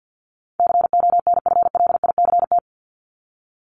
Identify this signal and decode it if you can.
Morse code, heard as a keyed tone, about 50 wpm, 700 Hz. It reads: CONFLICT
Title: Top 1001-1250 Words 50wpm